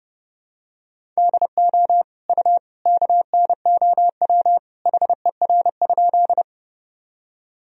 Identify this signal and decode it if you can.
Morse code, heard as a keyed tone, about 32 wpm, 700 Hz. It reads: DO U KNOW HER?
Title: Sentences from Top 100 Words and Abbreviations 30wpm